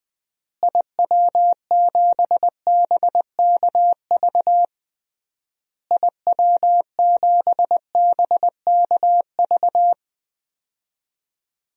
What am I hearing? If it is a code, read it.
Morse code, heard as a keyed tone, about 20 wpm, 700 Hz. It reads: IW7BKV IW7BKV
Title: DXCC-100 Call Signs - 6 Characters - Repeated 3x - 20wpm